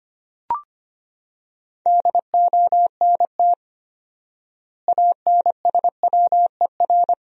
Morse code, heard as a keyed tone, about 25 wpm, 700 Hz. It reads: DONT ANSWER
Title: 4x Word-Spacing - Sentences from Top 400 Words - 25wpm